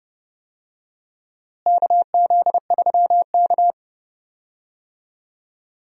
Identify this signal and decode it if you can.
Morse code, heard as a keyed tone, about 30 wpm, 700 Hz. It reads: KZ3K
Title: USA Call Signs - 4 Characters - Repeated 3x - 30wpm